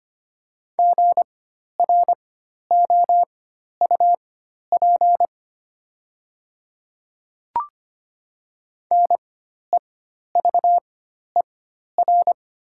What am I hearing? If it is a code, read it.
Morse code, heard as a keyed tone, about 25 wpm, spaced as for 12 wpm, 690 Hz. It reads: GROUP NEVER
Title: 4x Character Spacing - Top 300 Words - 25wpm